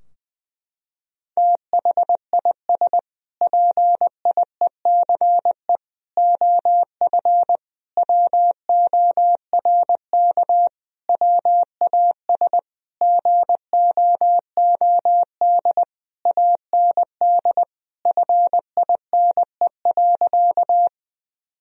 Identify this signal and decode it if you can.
Morse code, heard as a keyed tone, about 21 wpm, 700 Hz. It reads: THIS PIECE OF WORK WAS GOOD AND FINE.
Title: Sentences from Top 1000 Words - Review Part 2 - 20wpm